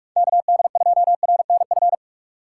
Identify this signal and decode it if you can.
Morse code, heard as a keyed tone, about 47 wpm, 700 Hz. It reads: KD2RNF
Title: USA Call Signs - 4x Character Spacing - 45wpm